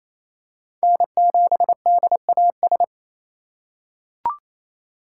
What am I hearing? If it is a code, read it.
Morse code, heard as a keyed tone, about 27 wpm, 700 Hz. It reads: N7DAS
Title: USA Call Signs - 4x Character Spacing - 28wpm